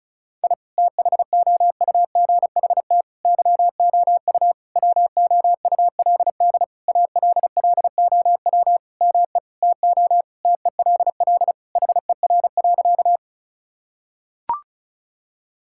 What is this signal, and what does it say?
Morse code, heard as a keyed tone, about 36 wpm, 700 Hz. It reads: I THOUGHT YOU WOULD ALLOW ME TO TELL HER.
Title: Sentences from Top 1000 Words - Review Part 5 - 35wpm